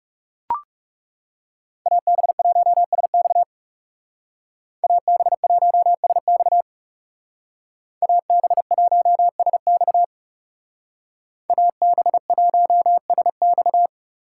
Morse code, heard as a keyed tone, about 35 wpm, 700 Hz. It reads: AB1SX AB1SX AB1SX AB1SX
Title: Speed Racing - USA Call Signs 30wpm